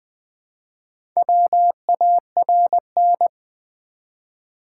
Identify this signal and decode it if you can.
Morse code, heard as a keyed tone, about 20 wpm, 700 Hz. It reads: WARN
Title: Four Letter Words - 20wpm